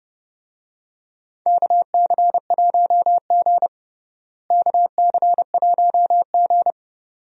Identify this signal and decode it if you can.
Morse code, heard as a keyed tone, about 31 wpm, 700 Hz. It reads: KC1G KC1G
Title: USA Call Signs - 4 Characters - Repeated 2x - 30wpm